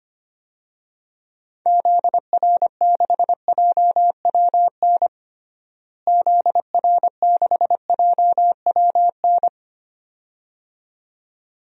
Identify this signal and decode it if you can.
Morse code, heard as a keyed tone, about 25 wpm, 700 Hz. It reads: ZR6JWN ZR6JWN
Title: DXCC-100 Call Signs - 6 Characters - Repeated 2x - 25wpm